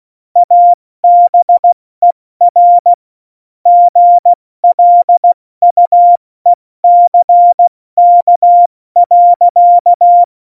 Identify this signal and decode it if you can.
Morse code, heard as a keyed tone, about 16 wpm, 700 Hz. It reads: ABER GLUECK.
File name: Morsekurs / Witze / Witz 15. Januar 2019, 70 F80